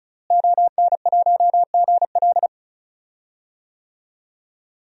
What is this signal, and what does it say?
Morse code, heard as a keyed tone, about 35 wpm, 700 Hz. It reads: ON1GL
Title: DXCC-100 Call Signs - 5 Characters - Repeated 2x - 35wpm